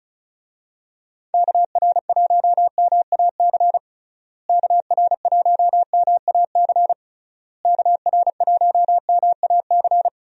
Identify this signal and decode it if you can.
Morse code, heard as a keyed tone, about 36 wpm, 700 Hz. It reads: KR1MAC KR1MAC KR1MAC
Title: USA Call Signs - 6 Characters - Repeated 3x - 35wpm